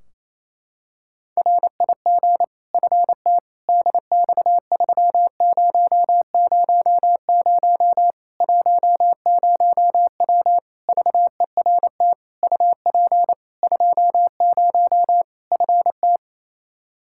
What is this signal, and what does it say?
Morse code, heard as a keyed tone, about 27 wpm, 700 Hz. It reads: RIG FT DX3000 10W VERT UP 20 FT
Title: Practice QSO - Exchange 2 - Info - 28wpm